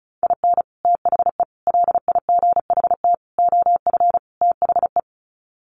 Morse code, heard as a keyed tone, about 34 wpm, 700 Hz. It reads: IN THE LIGHT OF THE
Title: Sets of 5 Words Encore2 35wpm